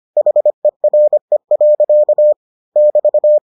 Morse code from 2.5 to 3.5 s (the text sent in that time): <BT>